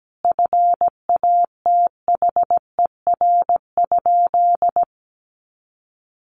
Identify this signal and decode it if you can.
Morse code, heard as a keyed tone, about 17 wpm, 700 Hz. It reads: FATHER?
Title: Sentences from Top 700 Words 17wpm